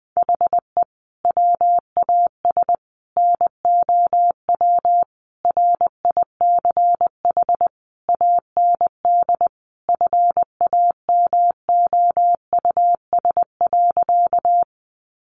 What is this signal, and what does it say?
Morse code, heard as a keyed tone, about 20 wpm, 700 Hz. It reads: HE WAS NOW RICH AND FAMOUS.
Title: Sentences from Top 1500 Words 20wpm